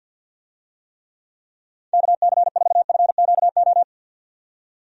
Morse code, heard as a keyed tone, about 50 wpm, 700 Hz. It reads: KC4FQO
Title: USA Call Signs - 6 Characters - 50wpm